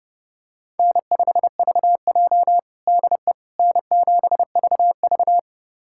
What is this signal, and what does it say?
Morse code, heard as a keyed tone, about 30 wpm, 700 Hz. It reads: N5VJ DE N7VV
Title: USA Callsign DE Callsign - 4 Characters - 30wpm